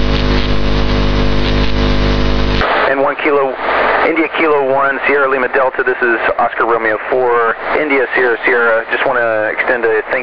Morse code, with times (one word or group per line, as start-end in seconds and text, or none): none